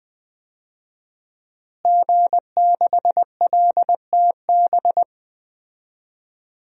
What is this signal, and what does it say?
Morse code, heard as a keyed tone, about 20 wpm, 700 Hz. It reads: G6LTB
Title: DXCC-100 Call Signs - 5 Characters - Repeated 2x - 20wpm